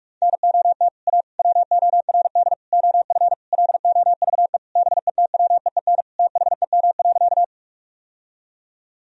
Morse code, heard as a keyed tone, about 47 wpm, 700 Hz. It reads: NOT A WORD OF LOVE BETWEEN THEM.
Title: Sentences from Top 1000 Words - Review Part 1 - 45wpm